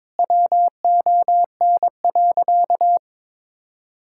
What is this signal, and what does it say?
Morse code, heard as a keyed tone, about 22 wpm, 700 Hz. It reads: WON.
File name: practice / 22wpm / Sentences with 2-Word Phrases - 22wpm